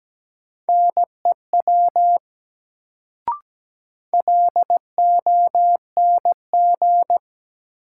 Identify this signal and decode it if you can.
Morse code, heard as a keyed tone, about 17 wpm, 700 Hz. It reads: NEW LONG
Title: Sets of 2 Words - New Word Drill - 17wpm